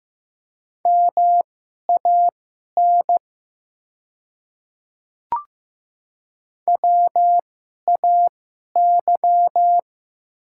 Morse code, heard as a keyed tone, about 15 wpm, spaced as for 11 wpm, 700 Hz. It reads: MAN WAY
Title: 2x Character Spacing - Top 300 Words - 15wpm